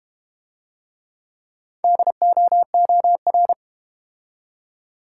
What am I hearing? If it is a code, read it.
Morse code, heard as a keyed tone, about 32 wpm, 700 Hz. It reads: DOOR